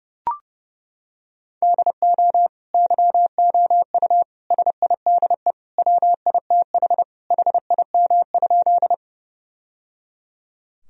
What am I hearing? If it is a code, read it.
Morse code, heard as a keyed tone, about 30 wpm, 700 Hz. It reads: DO YOU SIDE WITH HIM?
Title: Sentences from Top 1000 Words - Review Part 1 - 30wpm